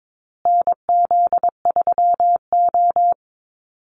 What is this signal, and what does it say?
Morse code, heard as a keyed tone, about 22 wpm, 700 Hz. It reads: NZ3O
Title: Call Signs Repeated 2x - 22wpm